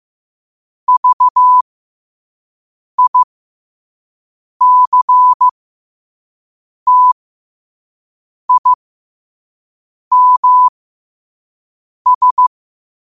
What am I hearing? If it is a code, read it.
Morse code, heard as a keyed tone, about 15 wpm, spaced as for 5 wpm, 1000 Hz. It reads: VICTIMS
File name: qotdfiles / 5wpm